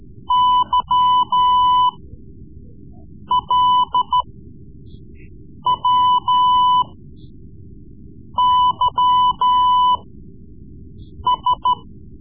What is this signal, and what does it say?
Morse code, heard as a keyed tone, about 13 wpm, spaced as for 5 wpm, 1000 Hz. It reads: YLWYS
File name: Morse Lessons / Morse Lessons / Tape-3B